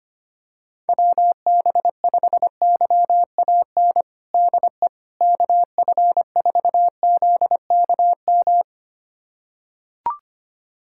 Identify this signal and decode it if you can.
Morse code, heard as a keyed tone, about 25 wpm, 700 Hz. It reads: WB5YAN DE KF4ZKM
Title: USA Callsign DE Callsign 25wpm